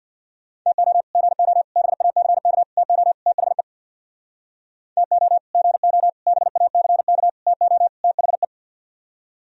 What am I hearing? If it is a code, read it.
Morse code, heard as a keyed tone, about 64 wpm, 700 Hz. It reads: TO GO BACK TO THE TO GO BACK TO THE